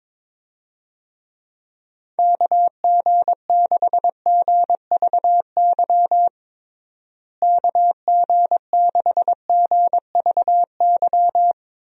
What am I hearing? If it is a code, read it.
Morse code, heard as a keyed tone, about 22 wpm, 700 Hz. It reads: KG6GVY KG6GVY
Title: USA Call Signs - 6 Characters - Repeated 3x - 22wpm